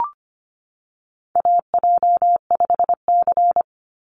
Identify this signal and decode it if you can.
Morse code, heard as a keyed tone, about 25 wpm, 700 Hz. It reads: AJ5C
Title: NA SOTA Chaser Callsigns 25wpm